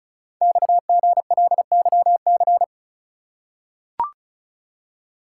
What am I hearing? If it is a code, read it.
Morse code, heard as a keyed tone, about 36 wpm, 690 Hz. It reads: XGLYC
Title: Code Groups - 5 Letters - 35wpm